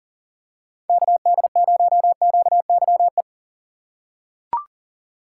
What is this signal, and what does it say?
Morse code, heard as a keyed tone, about 37 wpm, 700 Hz. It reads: KD0QYE
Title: USA Call Signs - 6 Characters - 6x Character Spacing - 40wpm